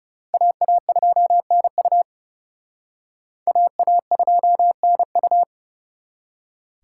Morse code, heard as a keyed tone, about 37 wpm, 700 Hz. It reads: AA2NU AA2NU
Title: Speed Racing - USA 2x2 Call Signs 30wpm